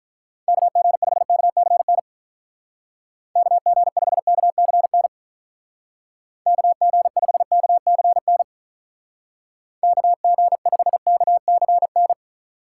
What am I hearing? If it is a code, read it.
Morse code, heard as a keyed tone, about 43 wpm, 700 Hz. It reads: KG5KCN KG5KCN KG5KCN KG5KCN
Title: Speed Racing - USA 2x3 Call Signs 35wpm